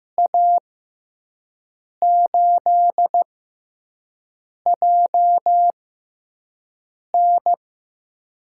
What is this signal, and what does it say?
Morse code, heard as a keyed tone, about 15 wpm, spaced as for 5 wpm, 700 Hz. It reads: A8JN
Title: USA Call Signs - 5 Characters - 6x Character Spacing - 15wpm